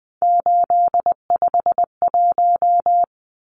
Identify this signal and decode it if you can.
Morse code, heard as a keyed tone, about 20 wpm, 700 Hz. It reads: 851
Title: Numbers - 3-Digit 20wpm